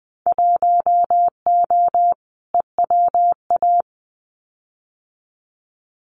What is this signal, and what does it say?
Morse code, heard as a keyed tone, about 19 wpm, 700 Hz. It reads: 1O EWA
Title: Winter Field Day Exchanges Encore 20wpm